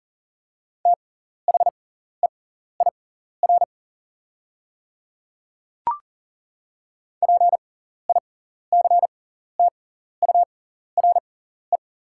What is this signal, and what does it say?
Morse code, heard as a keyed tone, about 40 wpm, spaced as for 14 wpm, 700 Hz. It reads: THEIR PICTURE
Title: 6x Character Spacing - Top 200 Words - 40wpm